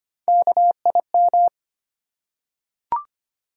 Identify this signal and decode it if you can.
Morse code, heard as a keyed tone, about 25 wpm, 700 Hz. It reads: KIM
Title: USA Common Names - Max 4 Letters - 25wpm